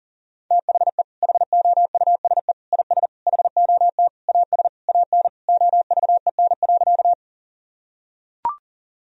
Morse code, heard as a keyed tone, about 40 wpm, 700 Hz. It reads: THE HOUSE IS HOT AS AN OVEN.
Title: Sentences from Top 1000 Words - Review Part 4 - 40wpm